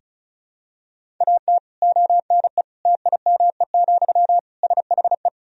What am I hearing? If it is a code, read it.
Morse code, heard as a keyed tone, about 36 wpm, 700 Hz. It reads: AT ONE TIME, SHE
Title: Sentences from Top 1000 Words - Review Part 1 - 35wpm